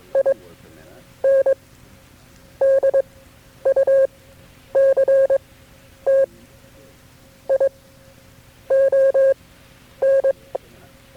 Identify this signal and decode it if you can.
Morse code, heard as a keyed tone, about 22 wpm, spaced as for 8 wpm, 540 Hz. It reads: INDUCTION
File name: FreeFiles / CW-practice / KB6MT-tape3-side1